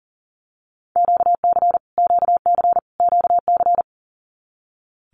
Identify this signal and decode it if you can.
Morse code, heard as a keyed tone, about 40 wpm, 700 Hz. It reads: QC QC QC
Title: Canadian Provinces and Territories - Repeated 3x - 40wpm